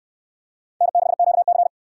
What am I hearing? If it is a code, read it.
Morse code, heard as a keyed tone, about 69 wpm, 700 Hz. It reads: N6QC